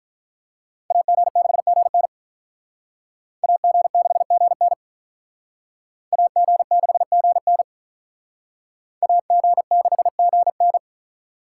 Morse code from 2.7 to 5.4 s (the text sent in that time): AG6GN